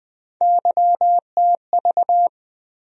Morse code, heard as a keyed tone, about 20 wpm, 700 Hz. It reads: YTV
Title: USA Call Sign Postfixes - 3 Letters 20wpm